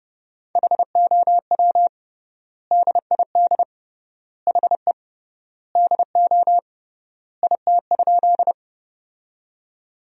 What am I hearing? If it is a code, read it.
Morse code, heard as a keyed tone, about 30 wpm, 700 Hz. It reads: HOW DID HE DO IT?
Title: 3x Word-Spacing - Sentences from Top 100 Words - 30wpm